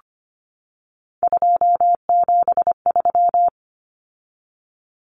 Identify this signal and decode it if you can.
Morse code, heard as a keyed tone, about 25 wpm, 690 Hz. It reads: 273
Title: Numbers - 3-Digit 25wpm